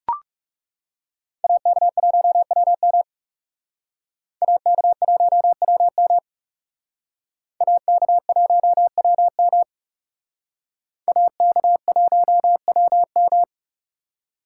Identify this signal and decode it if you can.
Morse code, heard as a keyed tone, about 37 wpm, 700 Hz. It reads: AK1WM AK1WM AK1WM AK1WM
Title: Speed Racing - USA 2x2 Call Signs 30wpm